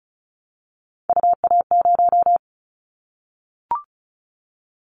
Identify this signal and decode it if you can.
Morse code, heard as a keyed tone, about 36 wpm, 700 Hz. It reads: UA0
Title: Callsign Prefixes for Top 100 DXCC - 35wpm